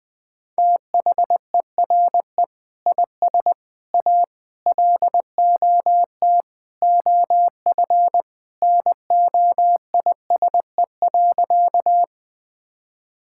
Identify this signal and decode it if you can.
Morse code, heard as a keyed tone, about 21 wpm, 700 Hz. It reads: THERE IS A LOT OF NOISE.
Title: Sentences with 2-Word Phrases - Spoken Beginning - 20wpm